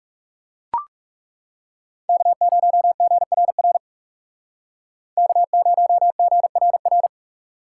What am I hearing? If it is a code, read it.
Morse code, heard as a keyed tone, about 35 wpm, 700 Hz. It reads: K0GRR K0GRR
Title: Speed Racing - USA 1x3 Call Signs 30wpm